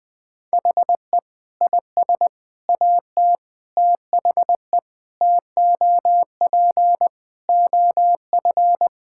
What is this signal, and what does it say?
Morse code, heard as a keyed tone, about 21 wpm, 700 Hz. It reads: HE IS AT THE TOP OF
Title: Sentences with 3-Word Phrases - Spoken Beginning - 20wpm